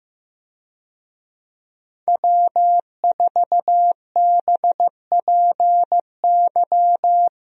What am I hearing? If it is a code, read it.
Morse code, heard as a keyed tone, about 15 wpm, 700 Hz. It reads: W4BPY
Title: USA Call Signs - 5 Characters - Repeated 3x - 15wpm